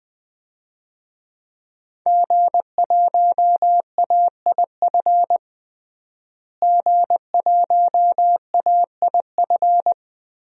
Morse code, heard as a keyed tone, about 21 wpm, 700 Hz. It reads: G1AIF G1AIF
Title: DXCC-100 Call Signs - 5 Characters - Repeated 3x - 20wpm